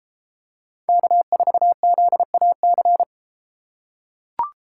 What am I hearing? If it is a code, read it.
Morse code, heard as a keyed tone, about 33 wpm, 700 Hz. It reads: K4ZAC